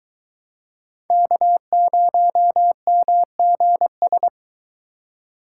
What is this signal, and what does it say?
Morse code, heard as a keyed tone, about 22 wpm, 700 Hz. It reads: K0MGS